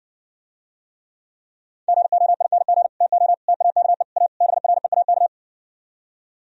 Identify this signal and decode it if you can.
Morse code, heard as a keyed tone, about 68 wpm, 700 Hz. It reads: GOING TO TAKE A BREAK